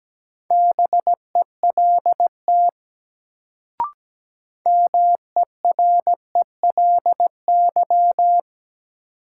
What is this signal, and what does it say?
Morse code, heard as a keyed tone, about 17 wpm, 700 Hz. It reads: BELT MERELY
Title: Sentences from Top 1400 Words - New Word Drill - 17wpm